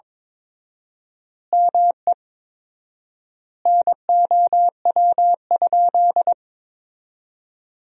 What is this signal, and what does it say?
Morse code, heard as a keyed tone, about 22 wpm, 700 Hz. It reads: ME NOW?
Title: 4x Word-Spacing - Sentences from Top 400 Words - 22wpm